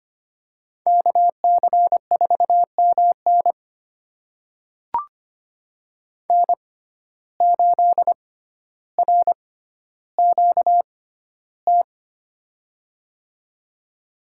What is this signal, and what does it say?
Morse code, heard as a keyed tone, about 25 wpm, spaced as for 13 wpm, 700 Hz. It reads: KC4MN N8RQT
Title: USA Call Signs - 5 Characters - 6x Character Spacing - 25wpm